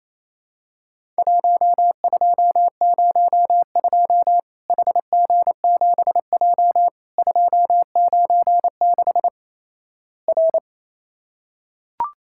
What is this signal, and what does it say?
Morse code, heard as a keyed tone, about 28 wpm, 700 Hz. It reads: 1202 HG7J 296 R
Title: Worked All Europe DX Contest - QTC Exchange - 28wpm